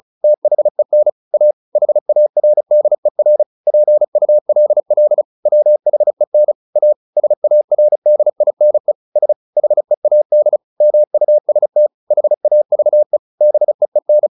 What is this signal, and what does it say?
Morse code, heard as a keyed tone, about 34 wpm, 600 Hz. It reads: THEN A HARDER PULL WHEN A SARDINE S HEAD MUST HAVE BEEN
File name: the old man and the sea / the old man and the sea 1 35wpm